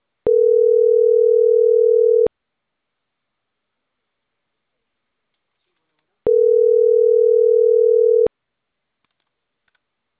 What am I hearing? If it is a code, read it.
Ringback tone (North America), 440 + 480 Hz.